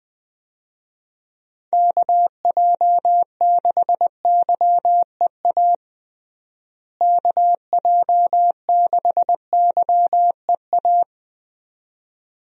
Morse code, heard as a keyed tone, about 20 wpm, 700 Hz. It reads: KJ6YEA KJ6YEA
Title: USA Call Signs - 6 Characters - Repeated 2x - 20wpm